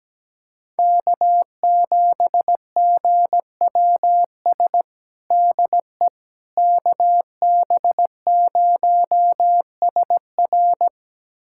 Morse code, heard as a keyed tone, about 17 wpm, 700 Hz. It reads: K7GWS DE KB0SR
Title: USA Callsign DE Callsign - 5 Characters - 17wpm